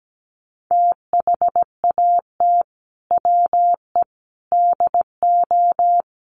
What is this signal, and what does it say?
Morse code, heard as a keyed tone, about 17 wpm, 700 Hz. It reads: THAT WE DO
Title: Sets of 5 Words Encore 17wpm